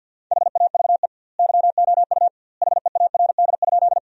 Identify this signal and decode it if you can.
Morse code, heard as a keyed tone, about 50 wpm, 700 Hz. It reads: HAVE YOU HEARD?
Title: 2x Word-Spacing - Sentences from Top 400 Words - 50wpm